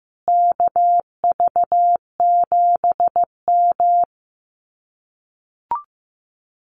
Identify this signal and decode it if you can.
Morse code, heard as a keyed tone, about 15 wpm, 700 Hz. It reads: KV7M
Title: Common QSO Phrases Encore 15wpm